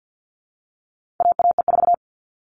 Morse code, heard as a keyed tone, about 50 wpm, 700 Hz. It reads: AAE4
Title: All Cut Numbers - 4-Digit 50wpm